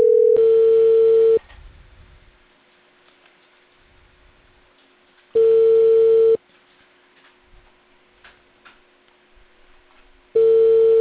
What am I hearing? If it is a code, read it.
Ringback tone (China, 450 Hz).